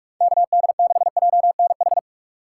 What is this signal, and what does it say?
Morse code, heard as a keyed tone, about 46 wpm, 700 Hz. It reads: KD6JNH
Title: USA Call Signs - 3x Character Spacing - 45wpm